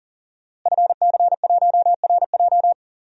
Morse code, heard as a keyed tone, about 40 wpm, 700 Hz. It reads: FC1RJ
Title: DXCC-100 Call Signs - 5 Characters - Repeated 2x - 40wpm